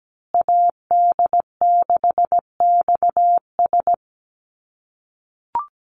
Morse code, heard as a keyed tone, about 17 wpm, 700 Hz. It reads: AD6XS
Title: USA Call Signs Repeated 2x 17wpm